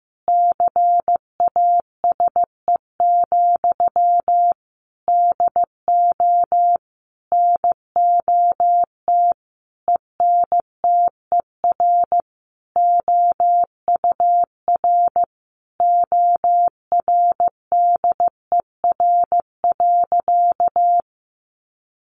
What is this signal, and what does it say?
Morse code, heard as a keyed tone, about 15 wpm, 700 Hz. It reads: CASE, DO NOT ENTER OUR ORDER.
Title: Sentences from Top 1300 Words 15wpm